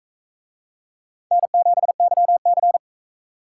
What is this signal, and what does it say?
Morse code, heard as a keyed tone, about 40 wpm, 700 Hz. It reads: N7YC